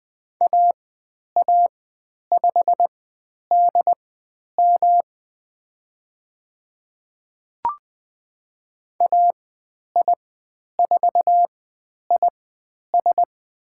Morse code, heard as a keyed tone, about 20 wpm, spaced as for 10 wpm, 700 Hz. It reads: AA5DM AI4IS
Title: 109 Lesson 18g - Letter m - Call Signs 20wpm